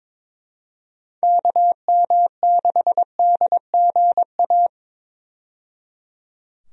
Morse code, heard as a keyed tone, about 22 wpm, 700 Hz. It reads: KM6DGA
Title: USA Call Signs - 6 Characters - 22wpm